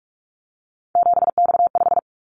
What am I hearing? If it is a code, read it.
Morse code, heard as a keyed tone, about 46 wpm, 700 Hz. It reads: 7X5
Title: Callsign Prefixes for Top 100 DXCC - 45wpm